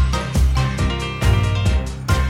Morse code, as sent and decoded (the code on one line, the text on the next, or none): none